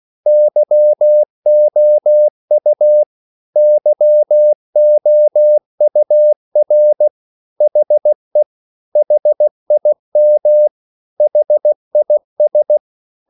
Morse code, heard as a keyed tone, about 16 wpm, 600 Hz. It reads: YOU YOUR HE HIM HIS
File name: CW / Practice / 500MostCommonEnglishWords16